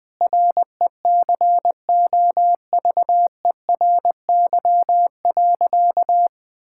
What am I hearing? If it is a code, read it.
Morse code, heard as a keyed tone, about 20 wpm, 700 Hz. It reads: RECOVERY.
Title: Sentences with 7-Word Phrases - 20wpm